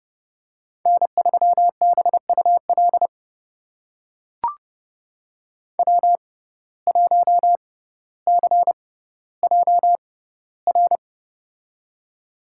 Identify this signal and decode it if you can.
Morse code, heard as a keyed tone, about 30 wpm, spaced as for 15 wpm, 700 Hz. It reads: N3BUL W1CJR
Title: USA Call Signs - 5 Characters - 6x Character Spacing - 30wpm